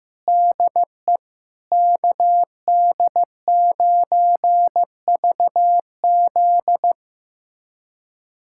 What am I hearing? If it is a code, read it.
Morse code, heard as a keyed tone, about 15 wpm, 700 Hz. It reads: DE KD9VZ
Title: USA Callsign DE Callsign - 5 Characters - 15wpm